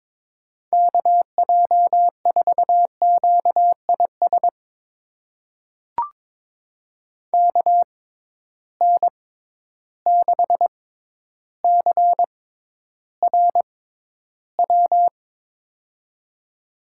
Morse code, heard as a keyed tone, about 22 wpm, spaced as for 11 wpm, 700 Hz. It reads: KJ4QIS KN6CRW
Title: USA Call Signs - 6 Characters - 6x Character Spacing - 22wpm